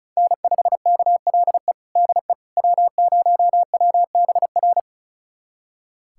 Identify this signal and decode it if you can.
Morse code, heard as a keyed tone, about 36 wpm, 700 Hz. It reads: N5KLE DE W0WBR
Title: USA Callsign DE Callsign - 5 Characters - 35wpm